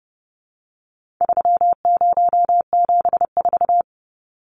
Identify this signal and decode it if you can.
Morse code, heard as a keyed tone, about 30 wpm, 700 Hz. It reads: 3074
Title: Numbers - 4-Digit 30wpm